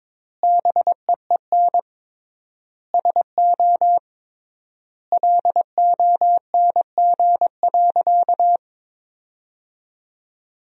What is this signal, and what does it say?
Morse code, heard as a keyed tone, about 22 wpm, 700 Hz. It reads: BEEN SO LONG.
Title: 3x Word-Spacing - Sentences from Top 100 Words - 22wpm